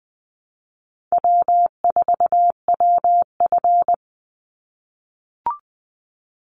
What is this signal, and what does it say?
Morse code, heard as a keyed tone, about 20 wpm, 700 Hz. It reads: W4WF
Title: Call Signs - 20wpm